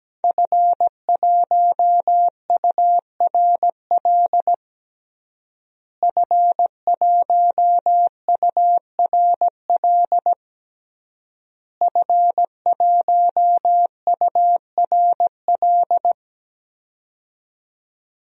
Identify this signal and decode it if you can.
Morse code, heard as a keyed tone, about 17 wpm, 700 Hz. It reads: F1URL F1URL F1URL
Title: DXCC-100 Call Signs - 5 Characters - Repeated 3x - 17wpm